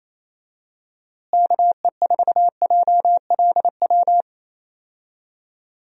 Morse code, heard as a keyed tone, about 28 wpm, 700 Hz. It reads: KE4JLW
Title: USA Call Signs - 6 Characters - Repeated 2x - 28wpm